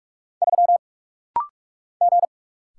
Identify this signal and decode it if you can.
Morse code, heard as a keyed tone, about 44 wpm, 690 Hz. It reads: 3 G
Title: Single Letter-Number - Rapid-Fire 45wpm